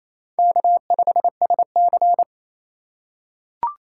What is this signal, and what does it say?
Morse code, heard as a keyed tone, about 27 wpm, 700 Hz. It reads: K5SC
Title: USA Call Signs - 2x Character Spacing - 28wpm